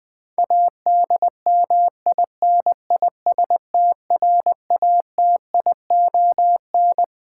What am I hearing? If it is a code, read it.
Morse code, heard as a keyed tone, about 20 wpm, 700 Hz. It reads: ADMINISTRATION
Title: Sets of 5 Words - New Word Drill - 20wpm